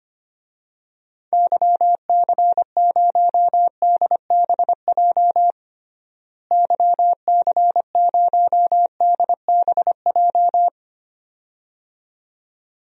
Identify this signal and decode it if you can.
Morse code, heard as a keyed tone, about 25 wpm, 700 Hz. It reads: YC0DBJ YC0DBJ
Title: DXCC-100 Call Signs - 6 Characters - Repeated 2x - 25wpm